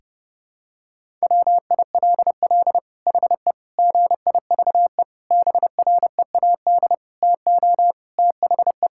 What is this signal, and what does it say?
Morse code, heard as a keyed tone, about 30 wpm, 700 Hz. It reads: WILL HE GIVE BREAD TO THE
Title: Sentences from Top 1600 Words 30wpm